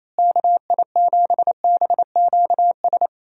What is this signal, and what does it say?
Morse code, heard as a keyed tone, about 28 wpm, 700 Hz. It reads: KI7BQS
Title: USA Call Signs - 6 Characters - Repeated 3x - 28wpm